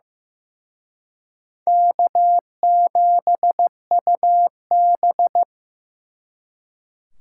Morse code, heard as a keyed tone, about 18 wpm, 700 Hz. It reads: K7UB
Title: Speed Racing - USA Call Signs 15wpm